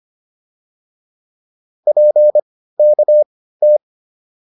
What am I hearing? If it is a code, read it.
Morse code, heard as a keyed tone, about 25 wpm, spaced as for 15 wpm, 600 Hz. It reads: PKT